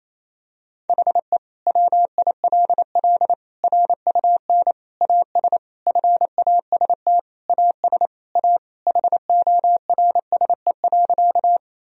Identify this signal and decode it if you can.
Morse code, heard as a keyed tone, about 28 wpm, 700 Hz. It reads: HE WILL RUN AS FAST AS A HORSE.
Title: Sentences from Top 1000 Words - Review Part 2 - 28wpm